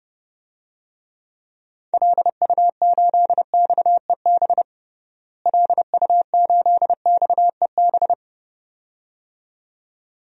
Morse code, heard as a keyed tone, about 31 wpm, 700 Hz. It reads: LU8XEB LU8XEB
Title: DXCC-100 Call Signs - 6 Characters - Repeated 2x - 30wpm